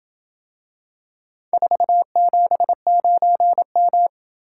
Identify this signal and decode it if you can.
Morse code, heard as a keyed tone, about 27 wpm, 700 Hz. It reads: 479M